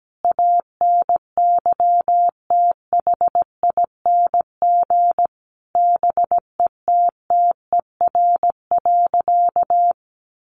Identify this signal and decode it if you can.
Morse code, heard as a keyed tone, about 17 wpm, 700 Hz. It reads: ANYTHING BETTER.
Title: Sentences from Top 1100 Words 17wpm